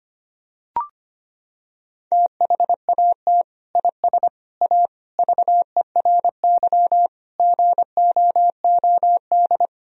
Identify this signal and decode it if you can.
Morse code, heard as a keyed tone, about 25 wpm, 700 Hz. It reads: THAT IS A VERY GOOD
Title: Sentences from Top 1000 Words - Review Part 4 - 25wpm